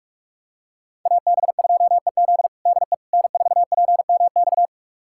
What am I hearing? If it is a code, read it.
Morse code, heard as a keyed tone, about 46 wpm, 700 Hz. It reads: AB2EZ DE N4PMX
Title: USA Callsign DE Callsign - 5 Characters - 45wpm